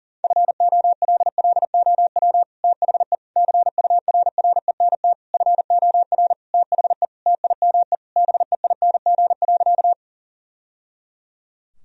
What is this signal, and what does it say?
Morse code, heard as a keyed tone, about 40 wpm, 700 Hz. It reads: FOLLOW THE CURRENT FOR THE TIME BEING.
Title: Sentences from Top 1000 Words - Review Part 4 - 40wpm